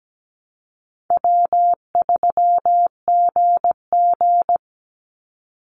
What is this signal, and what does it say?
Morse code, heard as a keyed tone, about 17 wpm, 700 Hz. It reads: W3GG
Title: Call Signs - 17wpm